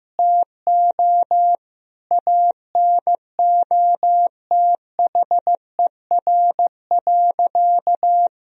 Morse code, heard as a keyed tone, about 16 wpm, 700 Hz. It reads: TO ANOTHER.
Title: Sentences from Top 1000 Words - Review Part 2 - 15wpm